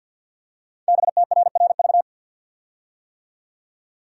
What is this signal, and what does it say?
Morse code, heard as a keyed tone, about 51 wpm, 700 Hz. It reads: BTRRV
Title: Code Groups - 5 Letters - 50wpm